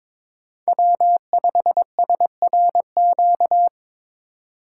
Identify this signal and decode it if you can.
Morse code, heard as a keyed tone, about 22 wpm, 700 Hz. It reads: W5SRQ
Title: USA Call Signs - 5 Characters - 3x Character Spacing - 22wpm